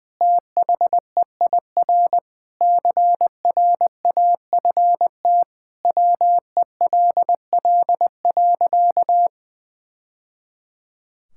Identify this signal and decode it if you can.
Morse code, heard as a keyed tone, about 21 wpm, 700 Hz. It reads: THEIR CRAFT WELL.
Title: Sentences from Top 1000 Words - Review Part 6 - 20wpm